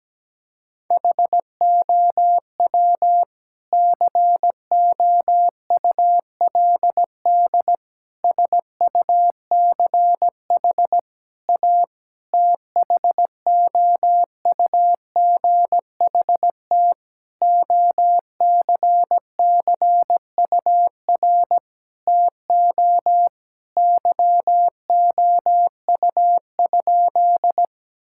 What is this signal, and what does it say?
Morse code, heard as a keyed tone, about 17 wpm, 700 Hz. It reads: HOW COULD SUCH A THOUGHT OCCUR TO YOU?
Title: Sentences from Top 1000 Words - Review Part 6 - 17wpm